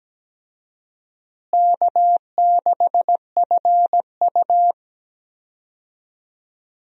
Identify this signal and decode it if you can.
Morse code, heard as a keyed tone, about 17 wpm, 700 Hz. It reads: K6FU
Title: USA Call Signs - 4 Characters - Repeated 3x - 17wpm